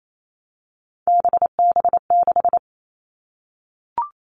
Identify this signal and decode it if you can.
Morse code, heard as a keyed tone, about 28 wpm, 700 Hz. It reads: BB6
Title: All Cut Numbers - 3-Digit 28wpm